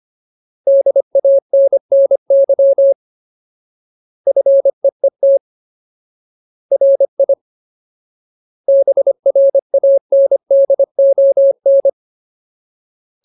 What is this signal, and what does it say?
Morse code, heard as a keyed tone, about 25 wpm, 550 Hz. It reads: DANNY FEET RI BRANDON
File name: Ultimate Word List / 25WPM Ultimate